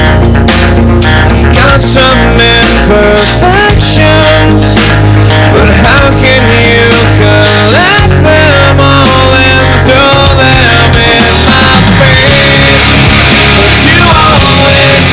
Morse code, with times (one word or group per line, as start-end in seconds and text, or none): none